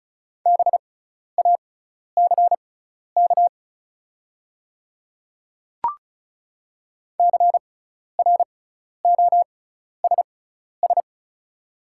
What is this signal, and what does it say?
Morse code, heard as a keyed tone, about 36 wpm, spaced as for 12 wpm, 700 Hz. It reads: BACK CROSS
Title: 6x Character Spacing - Top 300 Words - 35wpm